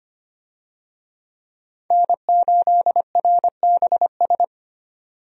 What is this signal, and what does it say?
Morse code, heard as a keyed tone, about 25 wpm, 700 Hz. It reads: N8RBS
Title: USA Call Signs - 5 Characters - 25wpm